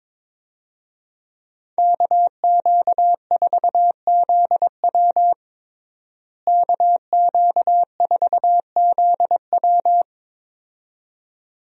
Morse code, heard as a keyed tone, about 22 wpm, 700 Hz. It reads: KQ4ZW KQ4ZW
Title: USA Call Signs - 5 Characters - Repeated 2x - 22wpm